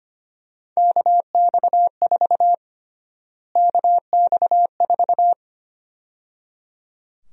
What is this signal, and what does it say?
Morse code, heard as a keyed tone, about 25 wpm, 700 Hz. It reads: KX4 KX4
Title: USA Call Sign Prefixes - Repeated 2x - 25wpm